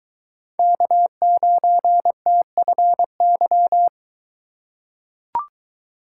Morse code, heard as a keyed tone, about 21 wpm, 700 Hz. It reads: K9TFY